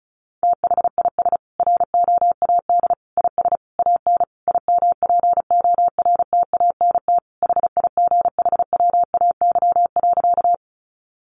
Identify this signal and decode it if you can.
Morse code, heard as a keyed tone, about 34 wpm, 700 Hz. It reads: THIS ROAD IS AN IMPORTANT HIGHWAY.
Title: Sentences from Top 1400 Words 35wpm